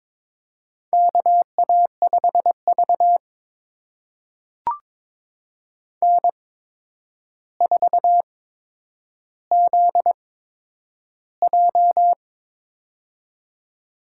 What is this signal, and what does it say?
Morse code, heard as a keyed tone, about 22 wpm, spaced as for 9 wpm, 700 Hz. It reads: KA5V N4ZJ
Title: USA Call Signs - 4 Characters - 8x Character Spacing - 22wpm